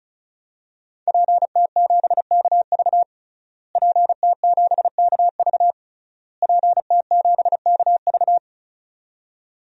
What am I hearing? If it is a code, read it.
Morse code, heard as a keyed tone, about 36 wpm, 700 Hz. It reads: PT7KV PT7KV PT7KV
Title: DXCC-100 Call Signs - 5 Characters - Repeated 3x - 35wpm